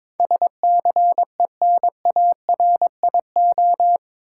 Morse code, heard as a keyed tone, about 22 wpm, 700 Hz. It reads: SCENARIO
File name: practice / 22wpm / Eight Letter Words - 22wpm